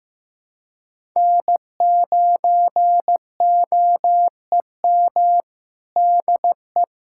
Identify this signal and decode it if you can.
Morse code, heard as a keyed tone, about 15 wpm, 700 Hz. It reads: N9OEM DE
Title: USA Callsign DE Callsign - 5 Characters - 15wpm